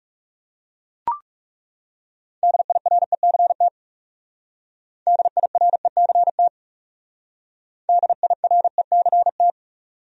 Morse code, heard as a keyed tone, about 35 wpm, 700 Hz. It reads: DIRECT DIRECT DIRECT
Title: Speed-Racing Top 500 Words 30wpm